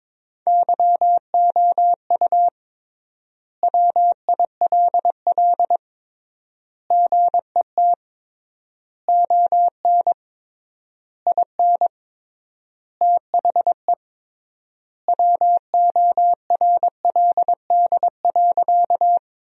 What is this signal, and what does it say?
Morse code, heard as a keyed tone, about 22 wpm, 700 Hz. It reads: YOU WILL GET ON IN THE WORLD.
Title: 3x Word-Spacing - Sentences from Top 200 Words - 22wpm